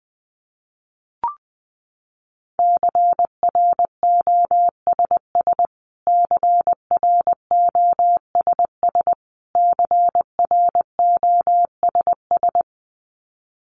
Morse code, heard as a keyed tone, about 20 wpm, 700 Hz. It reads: CROSS CROSS CROSS
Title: Top 500 Words Repeated 20wpm